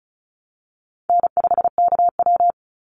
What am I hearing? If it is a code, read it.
Morse code, heard as a keyed tone, about 34 wpm, 700 Hz. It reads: N5KW
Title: Easy Call Signs Repeated 3x - 35wpm